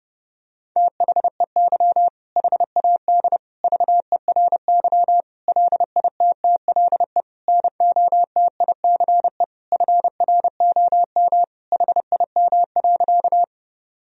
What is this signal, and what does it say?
Morse code, heard as a keyed tone, about 30 wpm, 700 Hz. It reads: THEY HAD VERY LITTLE NOTICE FROM HIM.
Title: Sentences from Top 1000 Words - Review Part 2 - 30wpm